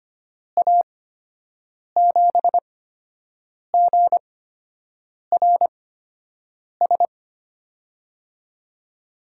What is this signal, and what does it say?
Morse code, heard as a keyed tone, about 25 wpm, spaced as for 7 wpm, 700 Hz. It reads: A7GRS
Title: USA Call Signs - 6 Characters - 8x Character Spacing - 25wpm